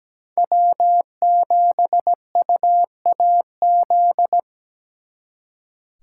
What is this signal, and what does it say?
Morse code, heard as a keyed tone, about 20 wpm, 700 Hz. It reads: W7UAZ
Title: Speed Racing - USA 1x3 Call Signs 17wpm